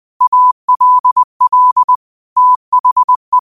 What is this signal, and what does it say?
Morse code, heard as a keyed tone, about 20 wpm, 1000 Hz. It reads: ALL THE